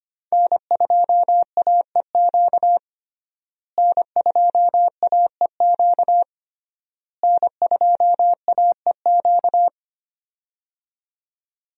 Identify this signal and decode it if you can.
Morse code, heard as a keyed tone, about 25 wpm, 700 Hz. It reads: N2AEQ N2AEQ N2AEQ
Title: USA Call Signs - 5 Characters - Repeated 3x - 25wpm